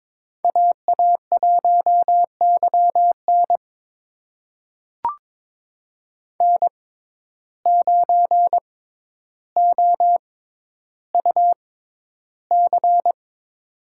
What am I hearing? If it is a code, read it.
Morse code, heard as a keyed tone, about 22 wpm, spaced as for 11 wpm, 700 Hz. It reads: AA1YN N9OUC
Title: USA Call Signs - 5 Characters - 6x Character Spacing - 22wpm